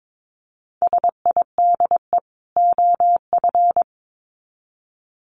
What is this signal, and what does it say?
Morse code, heard as a keyed tone, about 22 wpm, 700 Hz. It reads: SIDE OF
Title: Sets of 2 Words 22wpm